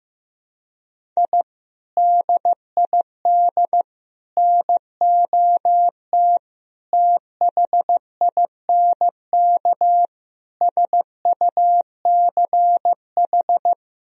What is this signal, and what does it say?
Morse code, heard as a keyed tone, about 15 wpm, 700 Hz. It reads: I DID NOT THINK SUCH
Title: Sentences from Top 1000 Words - Review Part 1 - 15wpm